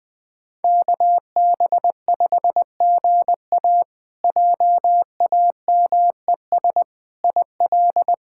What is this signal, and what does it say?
Morse code, heard as a keyed tone, about 20 wpm, 700 Hz. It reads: KB5GA JAMES IL
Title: K1USN SST QSO Practice - 20wpm